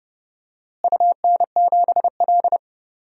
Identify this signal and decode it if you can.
Morse code, heard as a keyed tone, about 31 wpm, 700 Hz. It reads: UN7L
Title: DXCC-100 Call Signs - 4 Characters - Repeated 2x - 30wpm